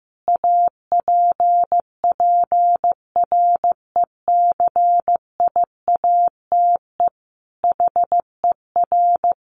Morse code, heard as a keyed tone, about 15 wpm, 700 Hz. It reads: APPRECIATE HER
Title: Sentences from Top 1300 Words 15wpm